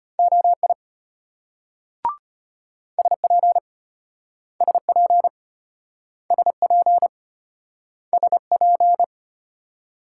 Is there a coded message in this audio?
Yes: Morse code from 0.2 to 9.0 s, heard as a keyed tone, about 33 wpm, 700 Hz. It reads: OI SP SP SP SP